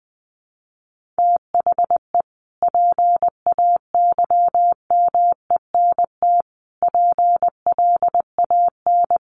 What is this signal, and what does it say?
Morse code, heard as a keyed tone, about 19 wpm, 700 Hz. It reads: THE PAYMENT PLAN
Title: Sentences from Top 1500 Words 20wpm